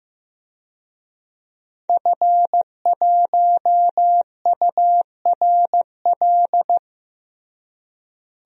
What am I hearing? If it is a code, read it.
Morse code, heard as a keyed tone, about 15 wpm, 700 Hz. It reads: F1URL
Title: DXCC-100 Call Signs - 5 Characters - Repeated 3x - 15wpm